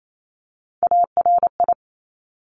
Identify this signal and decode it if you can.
Morse code, heard as a keyed tone, about 28 wpm, 700 Hz. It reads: ARI
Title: Top Three Letter Combinations in Words 28wpm